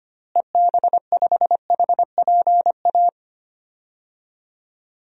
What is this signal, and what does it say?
Morse code, heard as a keyed tone, about 25 wpm, 700 Hz. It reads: EB5HPA
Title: DXCC-100 Call Signs - 6 Characters - Repeated 3x - 25wpm